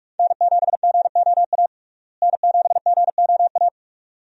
Morse code, heard as a keyed tone, about 46 wpm, 700 Hz. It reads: N7GOA N7GOA
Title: USA Call Signs - 5 Characters - Repeated 2x - 45wpm